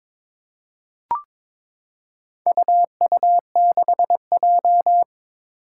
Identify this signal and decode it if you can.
Morse code, heard as a keyed tone, about 22 wpm, 700 Hz. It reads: UU6J
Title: Four Challenging Letter-Number Combinations 22wpm